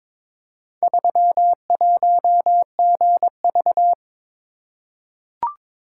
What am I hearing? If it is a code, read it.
Morse code, heard as a keyed tone, about 22 wpm, 700 Hz. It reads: 31GV
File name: practice / 22wpm / Four Letter-Number Combinations - 22wpm